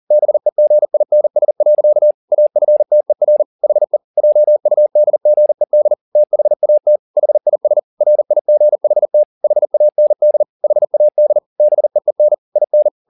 Morse code, heard as a keyed tone, about 40 wpm, 600 Hz. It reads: BEGINS. AFTER HE JUDGED THAT HIS RIGHT HAND HAD BEEN IN
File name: the old man and the sea 2 40wpm